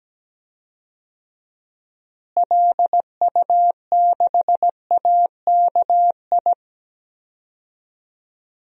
Morse code, heard as a keyed tone, about 17 wpm, 700 Hz. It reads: LU6AKI
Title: DXCC-100 Call Signs - 6 Characters - Repeated 3x - 17wpm